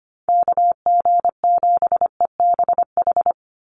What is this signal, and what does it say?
Morse code, heard as a keyed tone, about 25 wpm, 700 Hz. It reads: KG7EBH
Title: USA Call Signs 25wpm